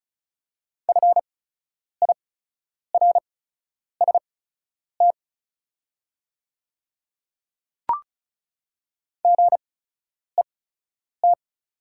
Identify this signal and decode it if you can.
Morse code, heard as a keyed tone, about 36 wpm, spaced as for 10 wpm, 690 Hz. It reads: FIRST GET
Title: 8x Character Spacing - Top 200 Words - 35wpm